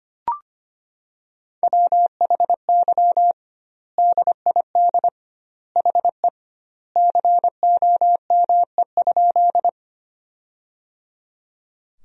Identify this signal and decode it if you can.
Morse code, heard as a keyed tone, about 25 wpm, 700 Hz. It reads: WHY DID HE COME?
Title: 2x Word-Spacing - Sentences from Top 200 Words - 25wpm